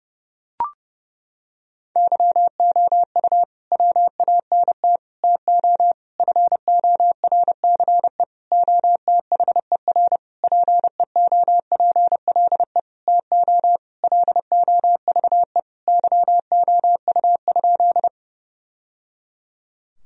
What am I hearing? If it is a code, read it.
Morse code, heard as a keyed tone, about 30 wpm, 700 Hz. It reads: YOU WANT TO FORCE OTHER PEOPLE TO LOVE YOU?
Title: Sentences from Top 1000 Words - Review Part 3 - 30wpm